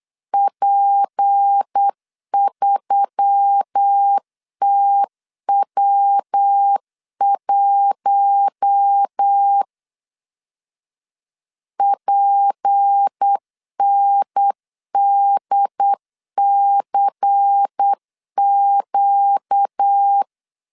Morse code, heard as a keyed tone, about 8 wpm, 800 Hz. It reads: P3TW1 PNDCQ
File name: LEZIONI / 6 PKX0 / 6 PKX 8wpm 3041